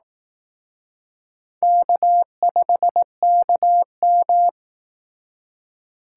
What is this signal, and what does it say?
Morse code, heard as a keyed tone, about 19 wpm, 700 Hz. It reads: K5KM